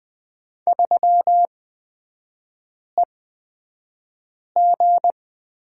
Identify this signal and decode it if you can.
Morse code, heard as a keyed tone, about 20 wpm, spaced as for 5 wpm, 700 Hz. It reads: 3EG
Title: 250 Lesson 40e - BK - Three Characters 20wpm